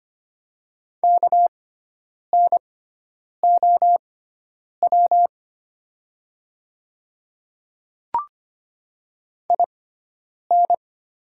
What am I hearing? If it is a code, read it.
Morse code, heard as a keyed tone, about 25 wpm, spaced as for 9 wpm, 700 Hz. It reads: KNOW IN
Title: 6x Character Spacing - Top 100 Words - 25wpm